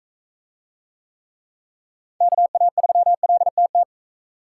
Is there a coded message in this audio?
Yes: Morse code from 2.2 to 3.8 s, heard as a keyed tone, about 39 wpm, 700 Hz. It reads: KA3LTT